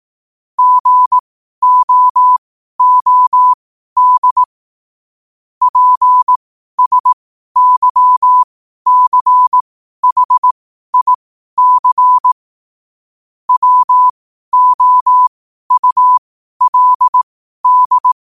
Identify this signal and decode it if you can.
Morse code, heard as a keyed tone, about 18 wpm, spaced as for 12 wpm, 1000 Hz. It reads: GOOD PSYCHIC WOULD
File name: qotdfiles / 12wpm